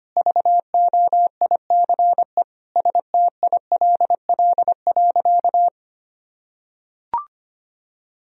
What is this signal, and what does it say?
Morse code, heard as a keyed tone, about 25 wpm, 700 Hz. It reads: VOICE STILL.
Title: Sentences from Top 500 Words and Abbreviations 25wpm